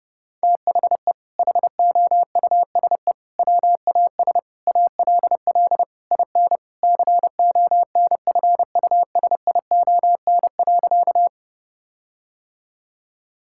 Morse code, heard as a keyed tone, about 30 wpm, 700 Hz. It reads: THE HOUSE WAS ALL IN CONFUSION.
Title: Sentences from Top 1600 Words 30wpm